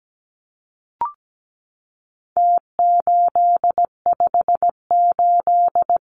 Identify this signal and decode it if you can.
Morse code, heard as a keyed tone, about 17 wpm, 700 Hz. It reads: T858
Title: Numbers - Contest Style - 4-Digit 17wpm